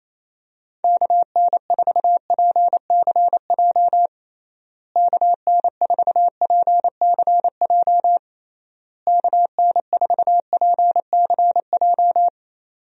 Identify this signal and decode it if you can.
Morse code, heard as a keyed tone, about 28 wpm, 700 Hz. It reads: KN4PCJ KN4PCJ KN4PCJ
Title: USA Call Signs - 6 Characters - Repeated 3x - 28wpm